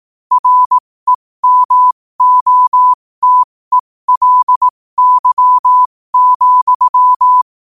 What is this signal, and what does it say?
Morse code, heard as a keyed tone, about 18 wpm, spaced as for 15 wpm, 1000 Hz. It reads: REMOTELY,
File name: qotdfiles / 15wpm